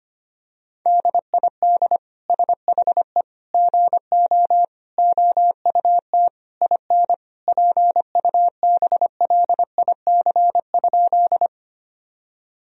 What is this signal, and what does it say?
Morse code, heard as a keyed tone, about 25 wpm, 700 Hz. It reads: DID SHE GO OUT IN PUBLIC?
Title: Sentences from Top 1000 Words - Review Part 3 - 25wpm